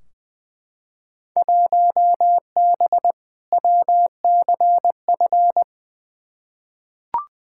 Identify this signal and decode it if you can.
Morse code, heard as a keyed tone, about 20 wpm, 700 Hz. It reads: 1B WCF
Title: Field Day Exchanges Encore - 20wpm